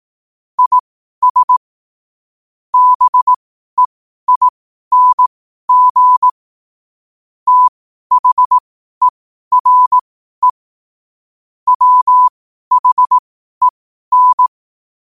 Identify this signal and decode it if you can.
Morse code, heard as a keyed tone, about 18 wpm, spaced as for 12 wpm, 1000 Hz. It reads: IS BEING THERE WHEN
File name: qotdfiles / 12wpm